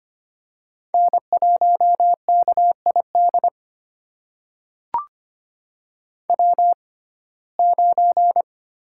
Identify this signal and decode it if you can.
Morse code, heard as a keyed tone, about 25 wpm, spaced as for 18 wpm, 700 Hz. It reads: N1KID W9
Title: USA Call Signs - 5 Characters - 6x Character Spacing - 25wpm